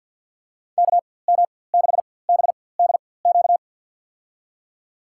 Morse code, heard as a keyed tone, about 48 wpm, spaced as for 23 wpm, 700 Hz. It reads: KM6BDQ
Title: USA Call Signs - 4x Character Spacing - 50wpm